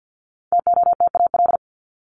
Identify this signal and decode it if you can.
Morse code, heard as a keyed tone, about 50 wpm, 700 Hz. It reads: TOTAL
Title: Top 1251-1500 Words 50wpm